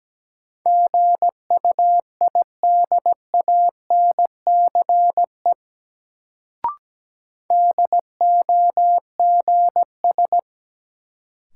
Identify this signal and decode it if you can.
Morse code, heard as a keyed tone, about 17 wpm, 700 Hz. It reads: GUIDANCE DOGS
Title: Sentences from Top 1600 Words - New Word Drill Alt - 17wpm